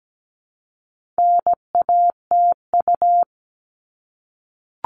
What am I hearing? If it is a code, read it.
Morse code, heard as a keyed tone, about 17 wpm, 700 Hz. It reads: NATU
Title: All Cut Numbers - 4-Digit 17wpm